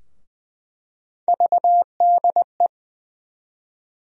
Morse code, heard as a keyed tone, about 20 wpm, 700 Hz. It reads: VDE
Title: USA Call Sign Postfixes - 3 Letters 20wpm